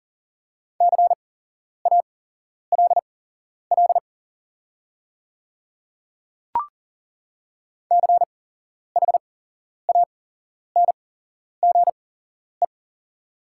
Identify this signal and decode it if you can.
Morse code, heard as a keyed tone, about 40 wpm, spaced as for 11 wpm, 690 Hz. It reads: CALL CHANGE
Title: 8x Character Spacing - Top 500 Words - 40wpm